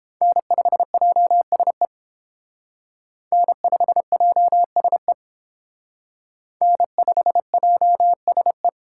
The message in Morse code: N5JSE N5JSE N5JSE

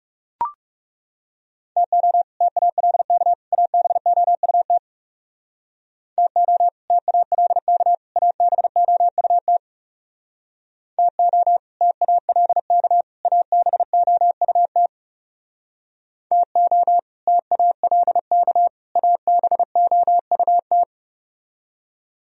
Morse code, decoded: TO TALK ABOUT TO TALK ABOUT TO TALK ABOUT TO TALK ABOUT

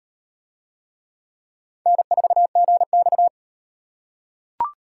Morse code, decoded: N4GX

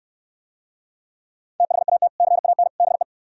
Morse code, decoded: THAT CAN BE

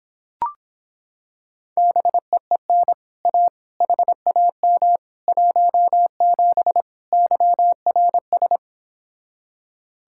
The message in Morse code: BEEN A HAM 17 YRS

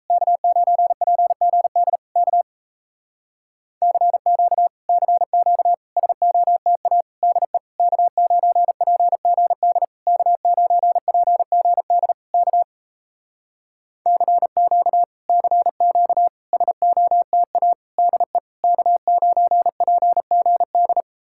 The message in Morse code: K9PGD K CQ CQ SOTA DE K9PGD K9PGD K CQ CQ SOTA DE K9PGD

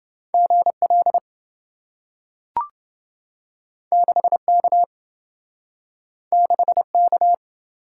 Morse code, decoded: GL 6K 6K

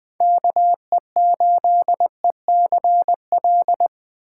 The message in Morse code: KE8ECL

